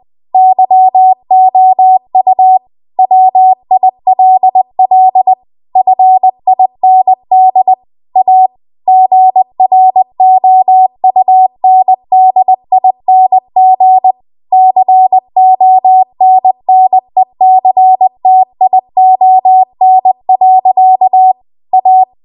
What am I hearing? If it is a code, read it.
Morse code, heard as a keyed tone, about 20 wpm, 750 Hz. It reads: YOU WILL FIND A GROUNDING CONNECTION. A
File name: Morse / 20WPM